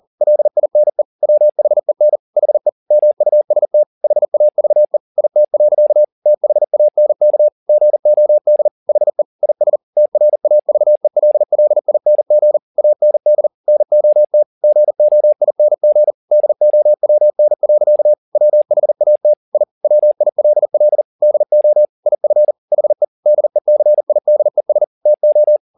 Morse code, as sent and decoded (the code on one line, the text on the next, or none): .-.. .. -. / .-- .... -. / .... / -- ..- ... - / .... .- ...- . / .. - .-.-.- / - .... .- -. -.- / --. --- -.. / .... / .. ... / - .-. .- ...- . .-.. .-.. .. -. --. / .- -. -.. / -. --- - / --. --- .. -. --. / -.. --- .-- -. .-.-.- / .-- .... .- - / .. / .-- .. .-.. .-.. / -.. --- / .. ..-. / .... / -.. -.-. .. -.. ... / - ---
LINE WHEN HE MUST HAVE IT. THANK GOD HE IS TRAVELLING AND NOT GOING DOWN. WHAT I WILL DO IF HE DECIDES TO